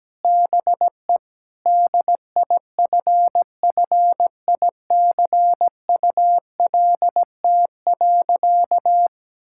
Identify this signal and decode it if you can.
Morse code, heard as a keyed tone, about 17 wpm, 700 Hz. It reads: BE DIFFICULT.
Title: Sentences from Top 1600 Words 17wpm